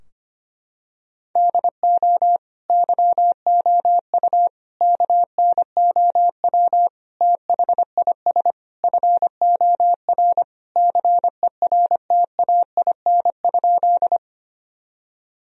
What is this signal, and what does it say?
Morse code, heard as a keyed tone, about 25 wpm, 700 Hz. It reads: DO YOU KNOW THIS FOR CERTAIN?
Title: Sentences from Top 1000 Words - Review Part 2 - 25wpm